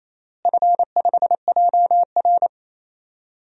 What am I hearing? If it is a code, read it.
Morse code, heard as a keyed tone, about 28 wpm, 700 Hz. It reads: F5JR
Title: DXCC-100 Call Signs - 4 Characters - Repeated 3x - 28wpm